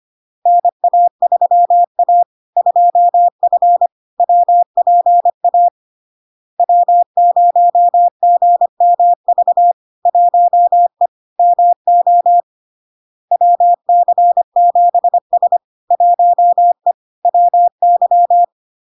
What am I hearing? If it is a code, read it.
Morse code, heard as a keyed tone, about 25 wpm, 700 Hz. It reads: NA3A 2F WPA W0GMV 1E MO WC7S 1E WY
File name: LARK-FD-2022-Log-in-CW-25wpm-700-Hz